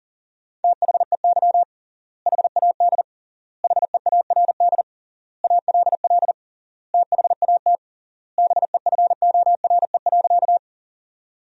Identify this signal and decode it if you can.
Morse code, heard as a keyed tone, about 40 wpm, 700 Hz. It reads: THEY HAD HEARD ALL THAT BEFORE.
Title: 3x Word-Spacing - Sentences from Top 400 Words - 40wpm